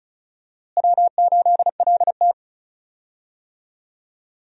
Morse code, heard as a keyed tone, about 36 wpm, 700 Hz. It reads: W8LT
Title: USA Call Signs - 4 Characters - Repeated 3x - 35wpm